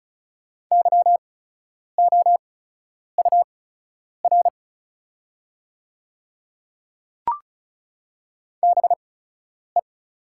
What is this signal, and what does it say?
Morse code, heard as a keyed tone, about 36 wpm, spaced as for 10 wpm, 700 Hz. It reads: YOUR BE
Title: 8x Character Spacing - Top 100 Words - 35wpm